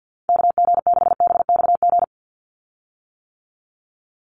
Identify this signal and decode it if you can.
Morse code, heard as a keyed tone, about 50 wpm, 700 Hz. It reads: KG6BXG
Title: USA Call Signs 50wpm